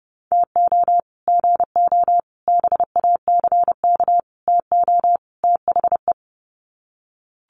Morse code, heard as a keyed tone, about 30 wpm, 700 Hz. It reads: TO GO BACK TO THE
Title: Sets of 5 Words 30wpm